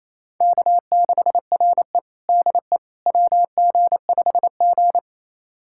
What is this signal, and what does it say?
Morse code, heard as a keyed tone, about 28 wpm, 700 Hz. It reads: K6RE DE WG5G
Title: USA Callsign DE Callsign - 4 Characters - 28wpm